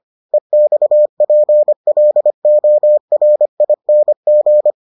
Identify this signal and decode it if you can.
Morse code, heard as a keyed tone, about 25 wpm, 600 Hz. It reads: EXPLORING